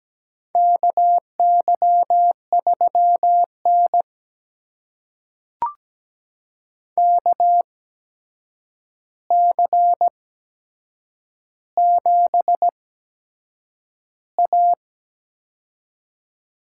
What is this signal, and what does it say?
Morse code, heard as a keyed tone, about 17 wpm, spaced as for 7 wpm, 700 Hz. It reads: KY3N KC7A
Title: USA Call Signs - 4 Characters - 8x Character Spacing - 17wpm